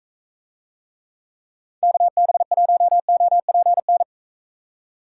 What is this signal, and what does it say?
Morse code, heard as a keyed tone, about 42 wpm, 700 Hz. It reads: KB1OPN